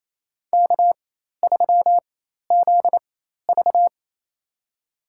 Morse code, heard as a keyed tone, about 27 wpm, spaced as for 13 wpm, 700 Hz. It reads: K3ZV
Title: USA Call Signs - 4 Characters - 4x Character Spacing - 28wpm